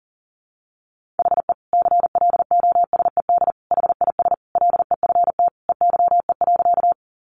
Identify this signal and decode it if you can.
Morse code, heard as a keyed tone, about 40 wpm, 700 Hz. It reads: HE CLOSED HIS LEFT EYE.
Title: Sentences from Top 1400 Words 40wpm